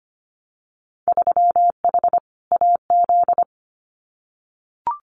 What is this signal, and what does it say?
Morse code, heard as a keyed tone, about 25 wpm, 700 Hz. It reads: 3H AZ
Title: Winter Field Day Exchanges Encore 25wpm